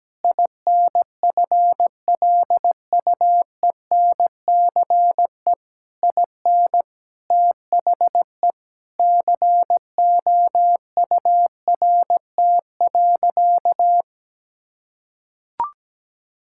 Morse code, heard as a keyed tone, about 17 wpm, 700 Hz. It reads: INFLUENCE IN THE COURT.
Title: Sentences from Top 1600 Words 17wpm